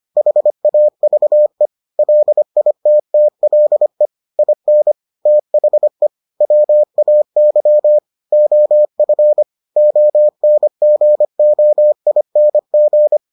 Morse code, heard as a keyed tone, about 25 wpm, 600 Hz. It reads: HAVE LITTLE IN THE WAY OF ONGOING